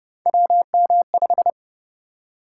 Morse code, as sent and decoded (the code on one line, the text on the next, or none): .-- -- .....
WM5